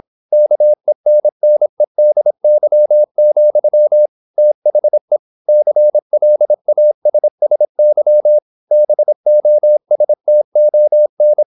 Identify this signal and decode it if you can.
Morse code, heard as a keyed tone, about 26 wpm, 600 Hz. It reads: KENNEDY, THE CLASSY BOSTON
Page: Never 26wpm | CW med Gnister